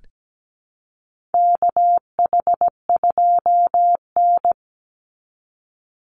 Morse code, heard as a keyed tone, about 17 wpm, 700 Hz. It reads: KH2N
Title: Call Signs Repeated 3x - 17wpm